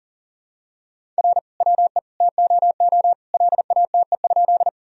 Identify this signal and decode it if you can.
Morse code, heard as a keyed tone, about 39 wpm, 700 Hz. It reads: R WE TOO LATE?
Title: Sentences from Top 300 Words and Abbreviations 40wpm